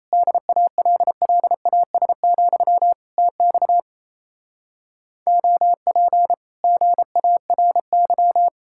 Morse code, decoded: DALLAS, TX OP GARY

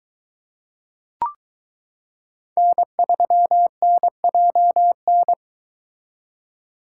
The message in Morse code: N3NJN

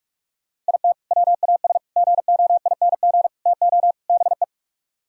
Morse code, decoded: IT WAS GOING TO BE

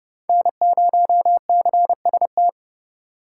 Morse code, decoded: N0CST

